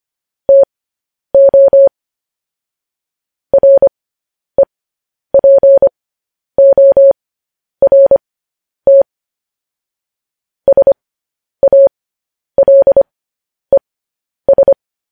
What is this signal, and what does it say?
Morse code, heard as a keyed tone, about 25 wpm, spaced as for 10 wpm, 550 Hz. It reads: TO REPORT SALES
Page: news-headlines-in-morse-code-at-10-wpm